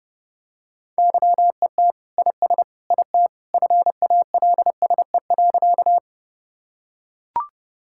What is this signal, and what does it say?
Morse code, heard as a keyed tone, about 30 wpm, 700 Hz. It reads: YET IS IT FALSE.
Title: Sentences from Top 1600 Words 30wpm